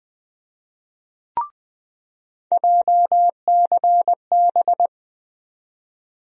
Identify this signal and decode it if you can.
Morse code, heard as a keyed tone, about 21 wpm, 700 Hz. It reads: JCB
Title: USA Call Sign Postfixes - 3 Letters 20wpm